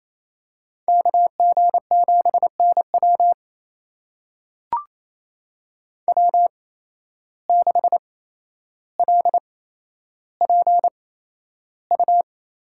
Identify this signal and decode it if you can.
Morse code, heard as a keyed tone, about 28 wpm, spaced as for 12 wpm, 700 Hz. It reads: KG7NW W6LPU
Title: USA Call Signs - 5 Characters - 8x Character Spacing - 28wpm